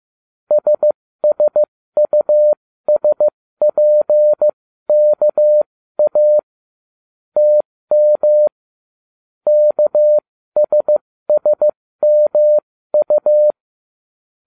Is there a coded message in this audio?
Yes: Morse code from 0.5 to 13.5 s, heard as a keyed tone, about 15 wpm, spaced as for 13 wpm, 600 Hz. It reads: SSUSPKA TM KSSMU